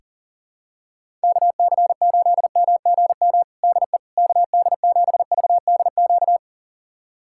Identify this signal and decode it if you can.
Morse code, heard as a keyed tone, about 40 wpm, 700 Hz. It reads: KC8MGM DE KD7VDQ
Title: USA Callsign DE Callsign - 6 Characters - 40wpm